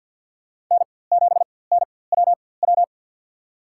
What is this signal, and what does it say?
Morse code, heard as a keyed tone, about 51 wpm, spaced as for 23 wpm, 700 Hz. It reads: N7NWW
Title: USA Call Signs - 5 Characters - 4x Character Spacing - 50wpm